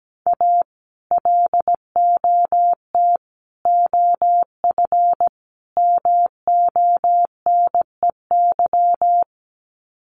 Morse code, heard as a keyed tone, about 17 wpm, 700 Hz. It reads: A LOT OF MONEY
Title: Sets of 4 Words 17wpm